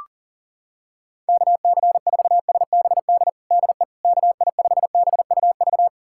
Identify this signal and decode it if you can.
Morse code, heard as a keyed tone, about 40 wpm, 700 Hz. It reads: KC4SBD DE KI5BUV
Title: USA Callsign DE Callsign - 6 Characters - 40wpm